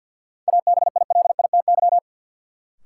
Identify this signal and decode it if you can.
Morse code, heard as a keyed tone, about 51 wpm, 700 Hz. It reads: ABILITY
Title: Sentences from Top 1000 Words - Review Part 3 - New Word Drill - 50wpm